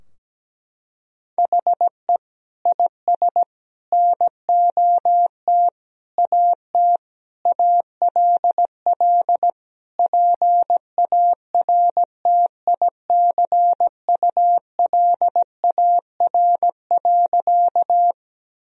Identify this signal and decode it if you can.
Morse code, heard as a keyed tone, about 17 wpm, 700 Hz. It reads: HE IS NOT AT ALL PARTICULAR.
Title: Sentences from Top 1000 Words - Review Part 4 - 17wpm